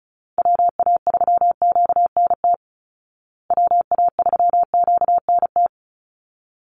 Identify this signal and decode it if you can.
Morse code, heard as a keyed tone, about 34 wpm, 700 Hz. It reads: WA3QNT WA3QNT
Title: Call Signs Repeated 3x - 35wpm